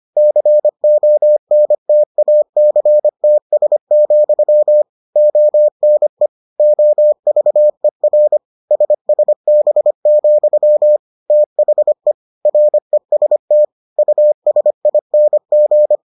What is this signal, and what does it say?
Morse code, heard as a keyed tone, about 25 wpm, 600 Hz. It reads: CONTACTS, ONE OVER SSB, THE REST USING